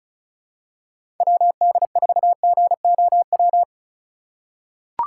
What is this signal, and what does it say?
Morse code, heard as a keyed tone, about 36 wpm, 700 Hz. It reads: WD4GOW
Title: USA Call Signs - 3x Character Spacing - 35wpm